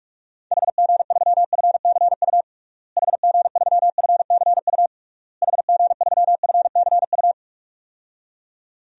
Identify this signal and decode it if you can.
Morse code, heard as a keyed tone, about 46 wpm, 700 Hz. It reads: HG3FCU HG3FCU HG3FCU
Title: DXCC-100 Call Signs - 6 Characters - Repeated 3x - 45wpm